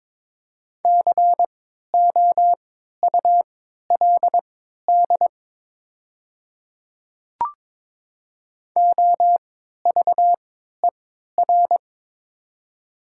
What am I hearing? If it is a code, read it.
Morse code, heard as a keyed tone, about 23 wpm, spaced as for 13 wpm, 700 Hz. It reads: COULD OVER
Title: 3x Character Spacing - Top 100 Words - 22wpm